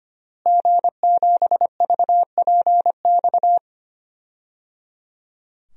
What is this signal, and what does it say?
Morse code, heard as a keyed tone, about 25 wpm, 700 Hz. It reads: G7VPX
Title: DXCC-100 Call Signs - 5 Characters - Repeated 3x - 25wpm